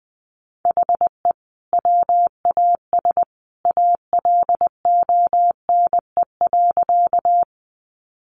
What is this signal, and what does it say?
Morse code, heard as a keyed tone, about 19 wpm, 700 Hz. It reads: HE WAS ALONE.
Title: Sentences from Top 1000 Words 20wpm